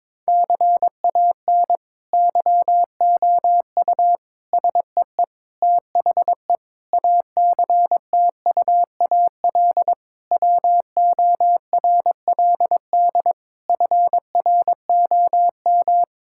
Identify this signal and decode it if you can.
Morse code, heard as a keyed tone, about 22 wpm, 700 Hz. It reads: CAN YOU SEE THE ACTUAL WORLD FROM
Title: Sentences from Top 1000 Words - Review Part 5 - 22wpm